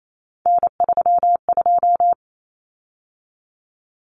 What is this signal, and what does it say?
Morse code, heard as a keyed tone, about 28 wpm, 700 Hz. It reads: N32
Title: Common Cut Numbers - 3-Digit 28wpm